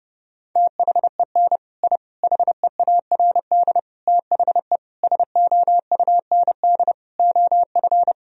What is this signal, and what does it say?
Morse code, heard as a keyed tone, about 30 wpm, 700 Hz. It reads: THEN I HEARD THE SOUND OF
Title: Sentences from Top 1000 Words - Review Part 1 - 30wpm